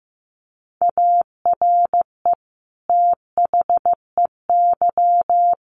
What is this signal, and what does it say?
Morse code, heard as a keyed tone, about 15 wpm, 700 Hz. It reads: ARE THEY
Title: Sentences from Top 1300 Words 15wpm